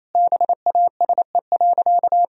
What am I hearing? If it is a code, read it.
Morse code, heard as a keyed tone, about 28 wpm, 700 Hz. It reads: BASE.
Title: Sentences from Top 1000 Words - Review Part 1 - 28wpm